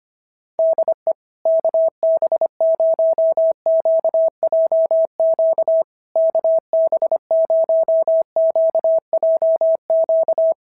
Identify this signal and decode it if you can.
Morse code, heard as a keyed tone, about 25 wpm, 660 Hz. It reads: DE KB0QJQ KB0QJQ
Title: SOTA QSO Practice 25wpm